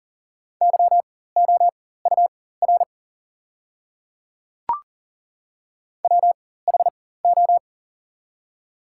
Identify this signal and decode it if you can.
Morse code, heard as a keyed tone, about 41 wpm, spaced as for 19 wpm, 690 Hz. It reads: YOUR WHO
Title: 4x Character Spacing - Top 100 Words - 40wpm